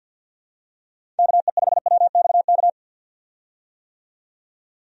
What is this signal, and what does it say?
Morse code, heard as a keyed tone, about 51 wpm, 700 Hz. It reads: KE5WXK
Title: USA Call Signs - 6 Characters - 50wpm